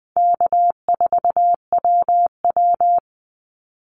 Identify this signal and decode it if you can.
Morse code, heard as a keyed tone, about 20 wpm, 700 Hz. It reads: K4WW
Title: Easy Call Signs Repeated 2x - 20wpm